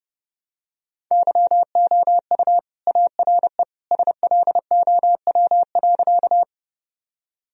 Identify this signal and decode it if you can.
Morse code, heard as a keyed tone, about 32 wpm, 700 Hz. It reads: YOU ARE SLOW.
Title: Sentences from Top 1000 Words - Review Part 2 - 30wpm